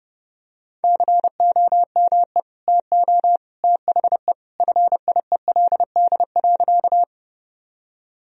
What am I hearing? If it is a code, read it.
Morse code, heard as a keyed tone, about 30 wpm, 700 Hz. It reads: COME TO THE FIELD.
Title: Sentences from Top 1000 Words - Review Part 2 - 30wpm